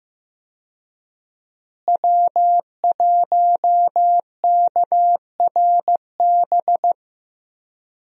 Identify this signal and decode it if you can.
Morse code, heard as a keyed tone, about 15 wpm, 700 Hz. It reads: W1KRB
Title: USA Call Signs - 5 Characters - Repeated 2x - 15wpm